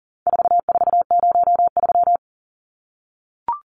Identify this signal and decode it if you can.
Morse code, heard as a keyed tone, about 38 wpm, 700 Hz. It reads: 4403
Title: Numbers - 4-Digit 40wpm